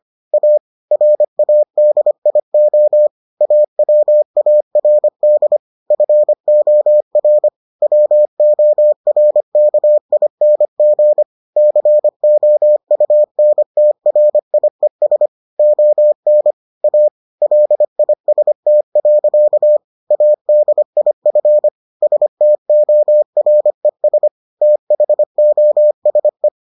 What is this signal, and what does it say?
Morse code, heard as a keyed tone, about 25 wpm, 600 Hz. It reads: A RADIO AWARD FOR WORKING COUNTRIES ON A LIST. ADIF STORES THOSE